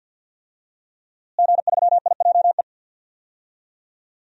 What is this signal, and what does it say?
Morse code, heard as a keyed tone, about 49 wpm, 700 Hz. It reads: G3IJE
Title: DXCC-100 Call Signs - 5 Characters - Repeated 2x - 50wpm